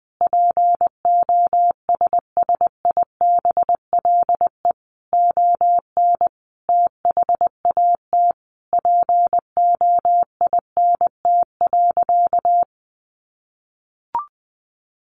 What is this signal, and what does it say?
Morse code, heard as a keyed tone, about 20 wpm, 700 Hz. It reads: POSSIBLE ON THAT POINT.
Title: Sentences from Top 1400 Words 20wpm